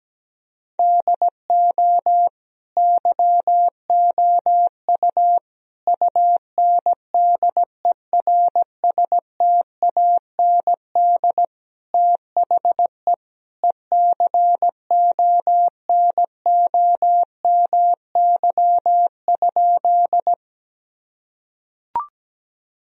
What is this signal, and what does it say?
Morse code, heard as a keyed tone, about 17 wpm, 700 Hz. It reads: DO YOU UNDERSTAND THE ECONOMY?
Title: Sentences from Top 1000 Words - Review Part 3 - 17wpm